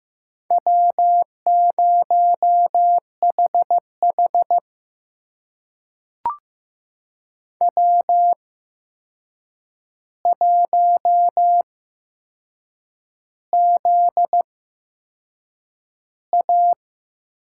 Morse code, heard as a keyed tone, about 15 wpm, spaced as for 6 wpm, 700 Hz. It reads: W0HH W1ZA
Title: USA Call Signs - 4 Characters - 8x Character Spacing - 15wpm